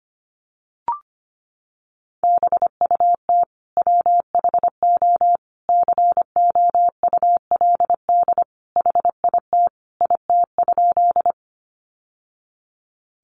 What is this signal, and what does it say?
Morse code, heard as a keyed tone, about 25 wpm, 700 Hz. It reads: BUT WHO COULD HIT IT?
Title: Sentences from Top 1200 Words 25wpm